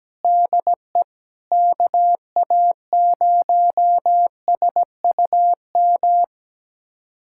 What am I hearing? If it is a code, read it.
Morse code, heard as a keyed tone, about 17 wpm, 700 Hz. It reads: DE KA0SUM
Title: USA Callsign DE Callsign - 6 Characters - 17wpm